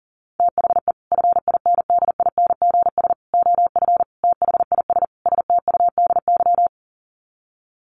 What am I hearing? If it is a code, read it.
Morse code, heard as a keyed tone, about 40 wpm, 700 Hz. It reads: THE FINDINGS OF THIS STUDY
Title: Sets of 5 Words Encore2 40wpm